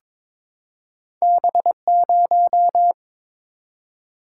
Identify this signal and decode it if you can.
Morse code, heard as a keyed tone, about 22 wpm, 700 Hz. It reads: B0
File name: practice / 22wpm / Two Letter-Number Combinations - 22wpm